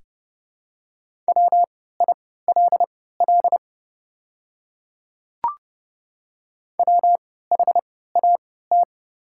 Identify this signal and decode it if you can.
Morse code, heard as a keyed tone, about 30 wpm, spaced as for 17 wpm, 700 Hz. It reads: WILL WHAT
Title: 3x Character Spacing - Top 100 Words - 30wpm